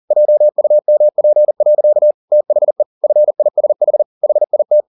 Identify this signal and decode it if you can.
Morse code, heard as a keyed tone, about 40 wpm, 600 Hz. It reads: JUMP. THE FISH HIT
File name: the old man and the sea 2 40wpm